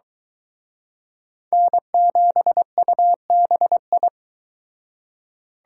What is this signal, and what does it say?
Morse code, heard as a keyed tone, about 23 wpm, 700 Hz. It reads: N7UBI